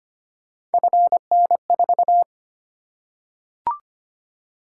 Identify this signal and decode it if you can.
Morse code, heard as a keyed tone, about 25 wpm, 700 Hz. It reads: FN4
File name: Three Letter-Number Combinations - 25wpm